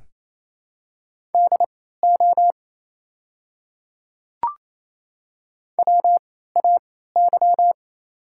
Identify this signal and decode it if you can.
Morse code, heard as a keyed tone, about 27 wpm, spaced as for 16 wpm, 700 Hz. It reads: DO WAY
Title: 3x Character Spacing - Top 200 Words - 28wpm